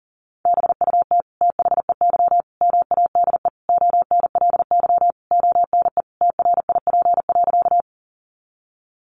Morse code, heard as a keyed tone, about 40 wpm, 700 Hz. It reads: BUT THEY MADE ONLY ONE TRIP.
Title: Sentences from Top 1300 Words 40wpm